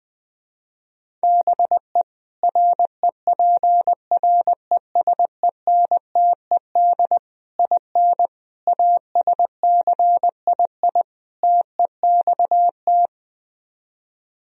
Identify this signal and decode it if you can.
Morse code, heard as a keyed tone, about 20 wpm, 700 Hz. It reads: BE REPRESENTED IN ASCII TEXT
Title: Speed-Racing - Sets of 5 Words 20wpm